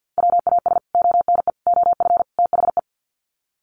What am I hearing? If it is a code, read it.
Morse code, heard as a keyed tone, about 50 wpm, 700 Hz. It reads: WAS ONE OF THE
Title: Sets of 4 Words 50wpm